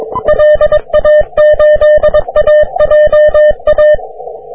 Morse code, heard as a keyed tone, about 24 wpm, 600 Hz. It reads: LA8AJA
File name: wp / tx5spa 40m cw